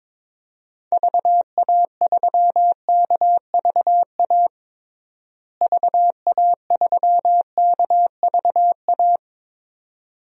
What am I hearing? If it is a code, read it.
Morse code, heard as a keyed tone, about 22 wpm, 700 Hz. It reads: VA3KVA VA3KVA
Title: DXCC-100 Call Signs - 6 Characters - Repeated 2x - 22wpm